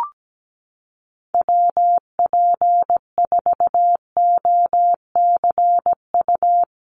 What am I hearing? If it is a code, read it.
Morse code, heard as a keyed tone, about 17 wpm, 700 Hz. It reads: WP4OCU
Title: USA Call Signs Repeated 3x 17wpm